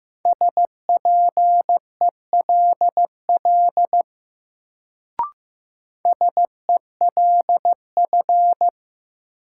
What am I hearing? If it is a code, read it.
Morse code, heard as a keyed tone, about 15 wpm, 700 Hz. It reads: SPELL SELF
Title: Sentences from Top 1000 Words - Review Part 1 - New Word Drill - 15wpm